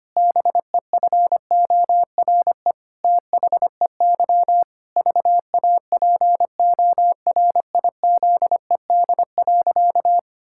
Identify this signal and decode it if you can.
Morse code, heard as a keyed tone, about 25 wpm, 700 Hz. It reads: BEFORE THEY VAPORIZED.
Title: Pangram Challenge v2 - 25wpm